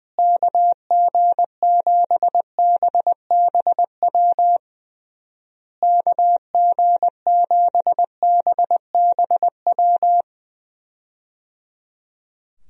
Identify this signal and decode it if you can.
Morse code, heard as a keyed tone, about 20 wpm, 700 Hz. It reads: KG7BBW KG7BBW
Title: USA Call Signs - 6 Characters - Repeated 3x - 20wpm